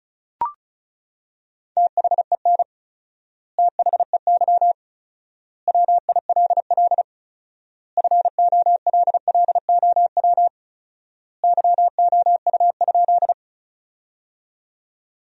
Morse code, heard as a keyed tone, about 36 wpm, 700 Hz. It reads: THEN THEY WILL FOLLOW YOU?
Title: 4x Word-Spacing - Sentences from Top 200 Words - 35wpm